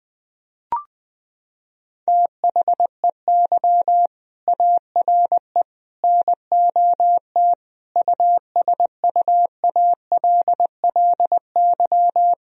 Morse code, heard as a keyed tone, about 21 wpm, 700 Hz. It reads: THEY ARE NOT USUALLY
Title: Sentences from Top 1000 Words - Review Part 3 - 20wpm